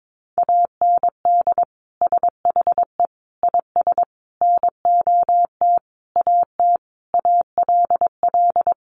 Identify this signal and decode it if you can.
Morse code, heard as a keyed tone, about 22 wpm, 700 Hz. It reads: AND SHE IS NOT AT ALL
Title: Sentences from Top 600 Words 22wpm